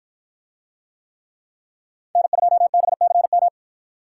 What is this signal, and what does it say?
Morse code, heard as a keyed tone, about 53 wpm, 700 Hz. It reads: N2BCM